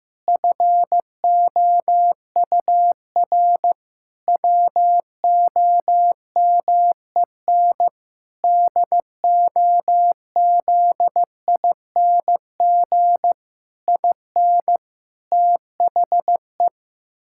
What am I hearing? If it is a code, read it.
Morse code, heard as a keyed tone, about 15 wpm, 700 Hz. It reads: FOUR WOMEN DOZING IN THE
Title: Pangram Challenge v2 - 15wpm